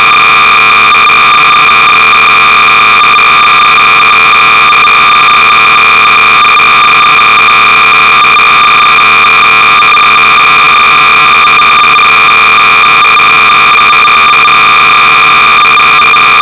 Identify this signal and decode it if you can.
Cassette data (Kansas City type).
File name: sorcerer / 300baud / Super-Star-Trek (V2.1)-(Sorcerers User Group-Modified by G.Lister)-BASIC-Sorcerer-300baud